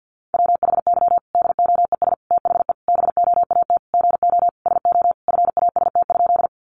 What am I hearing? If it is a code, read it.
Morse code, heard as a keyed tone, about 50 wpm, 700 Hz. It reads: WHY DOES THE BOAT GO SO FAST?
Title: Sentences from Top 500 Words 50wpm